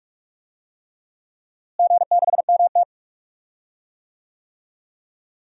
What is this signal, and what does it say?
Morse code, heard as a keyed tone, about 46 wpm, 700 Hz. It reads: G6MT
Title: DXCC-100 Call Signs - 4 Characters - Repeated 3x - 45wpm